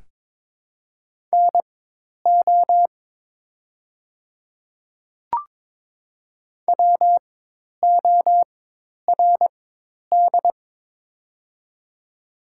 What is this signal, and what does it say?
Morse code, heard as a keyed tone, about 23 wpm, spaced as for 10 wpm, 700 Hz. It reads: NO WORD
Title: 4x Character Spacing - Top 100 Words - 22wpm